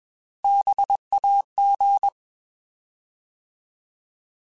Morse code, heard as a keyed tone, about 21 wpm, 800 Hz. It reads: BAG
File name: Alphabetical / Alphabetical mp3s / Bs @20